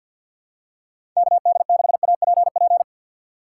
Morse code, heard as a keyed tone, about 53 wpm, 700 Hz. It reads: KD6APP
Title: USA Call Signs - 6 Characters - 6x Character Spacing - 50wpm